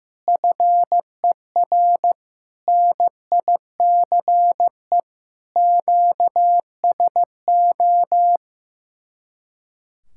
Morse code, heard as a keyed tone, about 15 wpm, 700 Hz. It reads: FER NICE QSO
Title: Speed-Racing Common QSO Phrases Encore - 15wpm